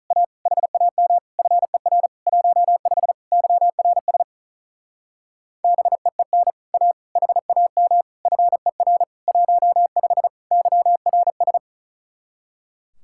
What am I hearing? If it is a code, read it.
Morse code, heard as a keyed tone, about 42 wpm, 700 Hz. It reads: A HAM FER 15 YRS BEEN A HAM FER 15 YRS
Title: Speed-Racing Common QSO Phrases Encore - 35wpm